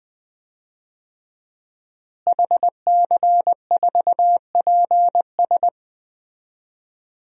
Morse code, heard as a keyed tone, about 20 wpm, 700 Hz. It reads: HC4PS
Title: DXCC-100 Call Signs - 5 Characters - Repeated 3x - 20wpm